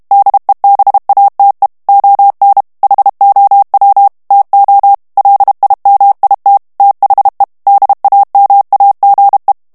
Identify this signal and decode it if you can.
Morse code, heard as a keyed tone, about 32 wpm, 800 Hz. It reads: DEBATE ON HOW TO LIMIT THE DAMAGE
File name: ANGLAIS texte clair 2 vitesse 30 mots minute